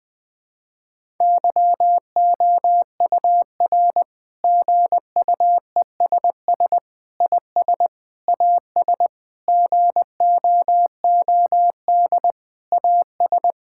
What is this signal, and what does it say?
Morse code, heard as a keyed tone, about 20 wpm, 700 Hz. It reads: YOUR GUESS IS AS GOOD AS
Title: Sentences with 7-Word Phrases - Spoken Beginning - 20wpm